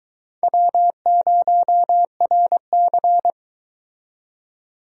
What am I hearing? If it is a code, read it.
Morse code, heard as a keyed tone, about 23 wpm, 700 Hz. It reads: W0RC